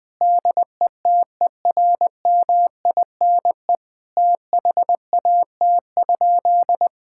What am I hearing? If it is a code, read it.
Morse code, heard as a keyed tone, about 20 wpm, 700 Hz. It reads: DETERMINE THAT?
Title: Sentences from Top 1000 Words - Review Part 5 - 20wpm